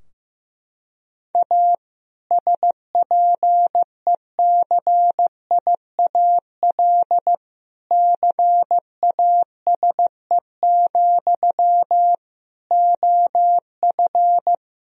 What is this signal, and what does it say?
Morse code, heard as a keyed tone, about 15 wpm, 700 Hz. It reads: A SPECIAL CASE, OF
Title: Sentences from Top 1000 Words - Review Part 3 - 15wpm